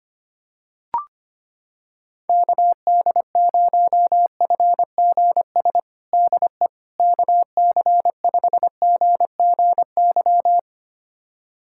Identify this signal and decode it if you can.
Morse code, heard as a keyed tone, about 25 wpm, 700 Hz. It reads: KD0FGS DE KC5GGY
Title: USA Callsign DE Callsign - 6 Characters - 25wpm